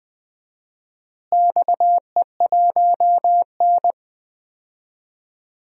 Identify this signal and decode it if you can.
Morse code, heard as a keyed tone, about 20 wpm, 700 Hz. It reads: XE1N
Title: DXCC-100 Call Signs - 4 Characters - Repeated 2x - 20wpm